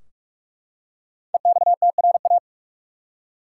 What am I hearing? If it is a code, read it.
Morse code, heard as a keyed tone, about 48 wpm, 700 Hz. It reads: EXTRA
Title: Five Letter Words - 45wpm